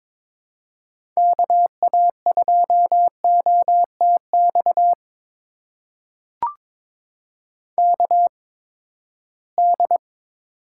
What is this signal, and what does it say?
Morse code, heard as a keyed tone, about 22 wpm, spaced as for 15 wpm, 700 Hz. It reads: KA2OTX KD
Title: USA Call Signs - 6 Characters - 8x Character Spacing - 22wpm